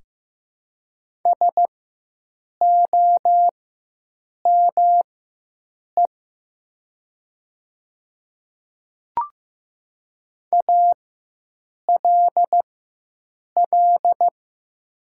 Morse code, heard as a keyed tone, about 15 wpm, spaced as for 7 wpm, 700 Hz. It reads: SOME ALL
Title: 4x Character Spacing - Top 100 Words - 15wpm